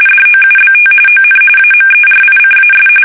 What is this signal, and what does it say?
RTTY, about 45.45 baud. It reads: DOUBT ON AMERICAS
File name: fsk850 / battleshipnj